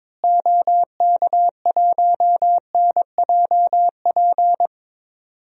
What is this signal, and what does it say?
Morse code, heard as a keyed tone, about 21 wpm, 700 Hz. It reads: OK1NJP
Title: DXCC-100 Call Signs - 6 Characters - Repeated 3x - 22wpm